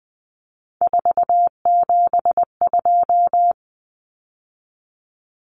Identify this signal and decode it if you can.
Morse code, heard as a keyed tone, about 20 wpm, 700 Hz. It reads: 472
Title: Numbers - 3-Digit 20wpm